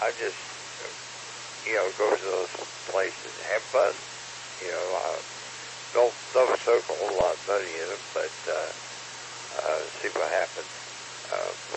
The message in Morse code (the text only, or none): none